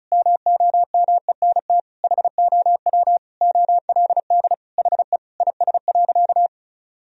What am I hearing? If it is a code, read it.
Morse code, heard as a keyed tone, about 36 wpm, 700 Hz. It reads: MOMENT HOW OLD HE IS.
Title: Sentences from Top 1000 Words - Review Part 4 - 35wpm